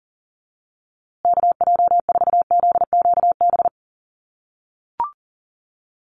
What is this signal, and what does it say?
Morse code, heard as a keyed tone, about 40 wpm, 700 Hz. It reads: KJ4ZQB
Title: USA Call Signs Repeated 2x 40wpm